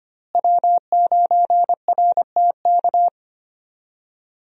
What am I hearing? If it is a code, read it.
Morse code, heard as a keyed tone, about 25 wpm, 700 Hz. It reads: W9RTK
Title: USA Call Signs - 5 Characters - Repeated 2x - 25wpm